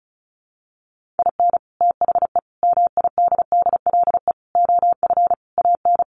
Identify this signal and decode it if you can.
Morse code, heard as a keyed tone, about 34 wpm, 700 Hz. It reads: IN THE MIDDLE OF AN
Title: Sets of 5 Words Encore2 35wpm